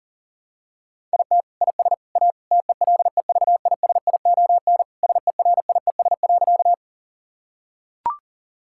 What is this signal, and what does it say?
Morse code, heard as a keyed tone, about 40 wpm, 700 Hz. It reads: IT IS A TELEVISION SERIES.
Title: Sentences from Top 1000 Words - Review Part 5 - 40wpm